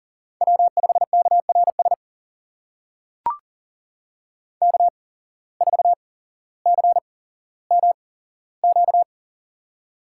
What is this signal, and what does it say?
Morse code, heard as a keyed tone, about 40 wpm, spaced as for 17 wpm, 700 Hz. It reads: W5KRS K4CMQ
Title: USA Call Signs - 5 Characters - 8x Character Spacing - 40wpm